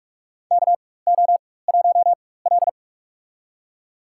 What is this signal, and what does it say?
Morse code, heard as a keyed tone, about 44 wpm, spaced as for 21 wpm, 700 Hz. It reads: KO1L
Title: USA Call Signs - 4 Characters - 4x Character Spacing - 45wpm